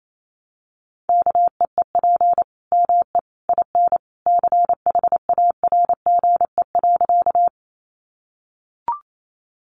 Morse code, heard as a keyed tone, about 28 wpm, 700 Hz. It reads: KEEP ME IN CHARGE.
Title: Sentences from Top 900 Words 28wpm